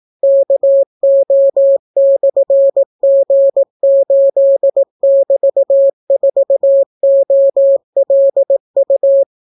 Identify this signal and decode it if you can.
Morse code, heard as a keyed tone, about 18 wpm, 550 Hz. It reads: KO/G8<BT>4OLU